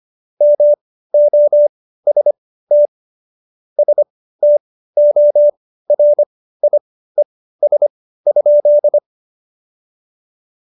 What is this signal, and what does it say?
Morse code, heard as a keyed tone, about 25 wpm, spaced as for 15 wpm, 590 Hz. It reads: MOST STORIES?
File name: Jokes-25-char-15-eff